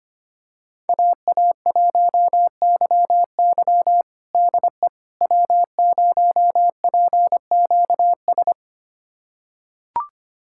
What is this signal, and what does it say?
Morse code, heard as a keyed tone, about 25 wpm, 700 Hz. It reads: AA1YY DE W0PQS
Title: USA Callsign DE Callsign - 5 Characters - 25wpm